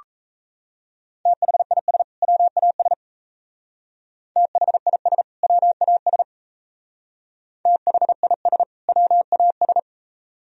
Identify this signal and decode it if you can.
Morse code, heard as a keyed tone, about 36 wpm, 700 Hz. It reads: THIS WAS THIS WAS THIS WAS